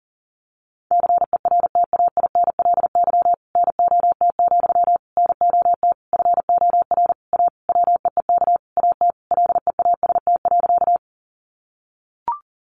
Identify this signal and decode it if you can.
Morse code, heard as a keyed tone, about 40 wpm, 700 Hz. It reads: CERTAINLY NOT, NOT FOR A WEEK AT LEAST.
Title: Sentences from Top 900 Words 40wpm